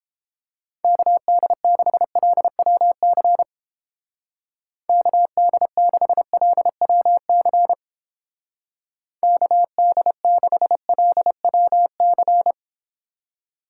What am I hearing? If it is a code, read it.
Morse code, heard as a keyed tone, about 31 wpm, 700 Hz. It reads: KD6LWC KD6LWC KD6LWC